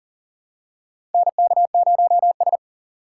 Morse code, heard as a keyed tone, about 40 wpm, 700 Hz. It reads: NK0S
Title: USA Call Signs - 4x Character Spacing - 40wpm